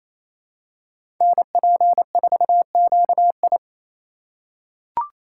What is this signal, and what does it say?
Morse code, heard as a keyed tone, about 28 wpm, 700 Hz. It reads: NP4QI
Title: USA Call Signs - 5 Characters - 4x Character Spacing - 28wpm